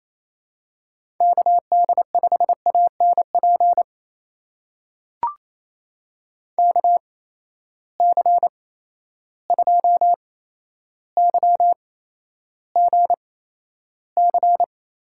Morse code, heard as a keyed tone, about 27 wpm, spaced as for 12 wpm, 700 Hz. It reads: KD5ANP KC2YGC
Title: USA Call Signs - 6 Characters - 8x Character Spacing - 28wpm